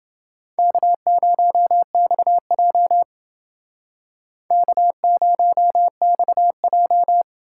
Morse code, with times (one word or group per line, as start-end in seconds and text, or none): 0.6-3.0 K0XJ
4.5-7.2 K0XJ